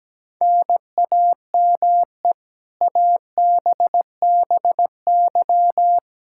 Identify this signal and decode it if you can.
Morse code, heard as a keyed tone, about 17 wpm, 700 Hz. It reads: NAME ABBY
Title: Practice QSO - Exchange 1 - Beginning - 17wpm